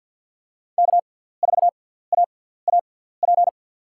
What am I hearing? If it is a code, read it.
Morse code, heard as a keyed tone, about 47 wpm, spaced as for 17 wpm, 700 Hz. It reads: K4AAP
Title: USA Call Signs - 5 Characters - 6x Character Spacing - 50wpm